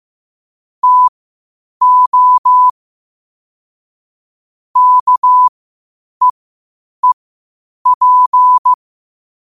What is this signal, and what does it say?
Morse code, heard as a keyed tone, about 15 wpm, spaced as for 8 wpm, 1000 Hz. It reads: TO KEEP
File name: qotdfiles / 8wpm